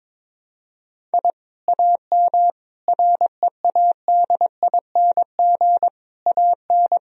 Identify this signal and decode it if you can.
Morse code, heard as a keyed tone, about 22 wpm, 700 Hz. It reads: I AM READING AN
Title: Sentences from Top 1000 Words - Review Part 4 - 22wpm